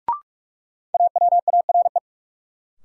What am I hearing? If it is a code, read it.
Morse code, heard as a keyed tone, about 46 wpm, 700 Hz. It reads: AWARE
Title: Sentences from Top 1200 Words - New Word Drill Alt - 45wpm